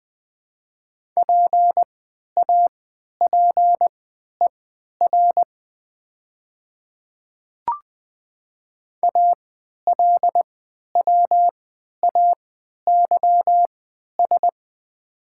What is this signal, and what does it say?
Morse code, heard as a keyed tone, about 20 wpm, spaced as for 11 wpm, 700 Hz. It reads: PAPER ALWAYS
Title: 3x Character Spacing - Top 300 Words - 20wpm